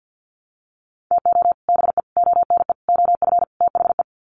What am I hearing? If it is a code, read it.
Morse code, heard as a keyed tone, about 50 wpm, 700 Hz. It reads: TO BE ONE OF THE
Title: Sets of 5 Words 50wpm